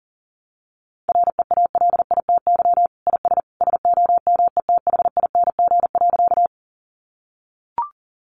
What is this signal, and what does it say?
Morse code, heard as a keyed tone, about 40 wpm, 700 Hz. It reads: REALITY IS SOMETHING.
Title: Sentences from Top 1200 Words 40wpm